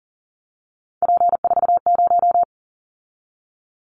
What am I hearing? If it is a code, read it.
Morse code, heard as a keyed tone, about 37 wpm, 690 Hz. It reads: P40
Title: Callsign Prefixes for Top 100 DXCC - 40wpm